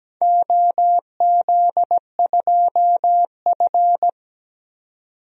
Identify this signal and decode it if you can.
Morse code, heard as a keyed tone, about 17 wpm, 700 Hz. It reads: OZ2F
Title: DXCC-100 Call Signs - 4 Characters - 17wpm